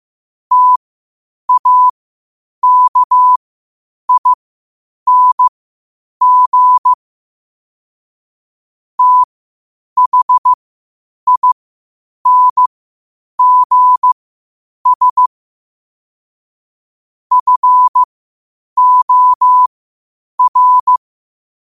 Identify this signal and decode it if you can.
Morse code, heard as a keyed tone, about 15 wpm, spaced as for 8 wpm, 1000 Hz. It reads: TAKING THINGS FOR
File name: qotdfiles / 8wpm